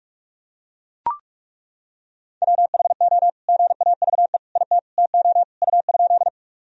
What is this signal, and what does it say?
Morse code, heard as a keyed tone, about 42 wpm, 690 Hz. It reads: WHO GAVE IT TO U?
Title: Sentences from Top 500 Words and Abbreviations 45wpm